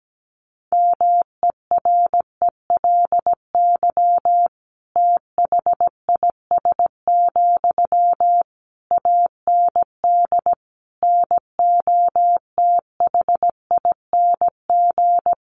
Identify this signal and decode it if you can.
Morse code, heard as a keyed tone, about 17 wpm, 700 Hz. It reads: MERELY THIS, AND NOTHING
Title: Sentences from Top 1400 Words 17wpm